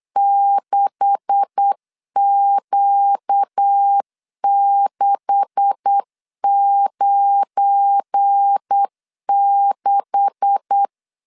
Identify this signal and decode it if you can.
Morse code, heard as a keyed tone, about 8 wpm, 800 Hz. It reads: 6Q696
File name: LEZIONI / 9 286 8wpm 3425